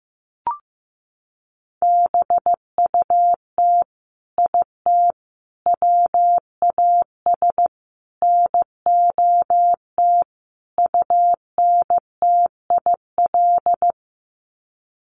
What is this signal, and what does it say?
Morse code, heard as a keyed tone, about 15 wpm, 700 Hz. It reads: BUT IT WAS NOT UNTIL
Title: Sets of 5 Words Encore2 15wpm